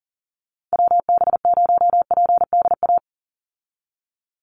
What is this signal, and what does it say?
Morse code, heard as a keyed tone, about 38 wpm, 700 Hz. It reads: WB0PDA
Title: USA Call Signs 40wpm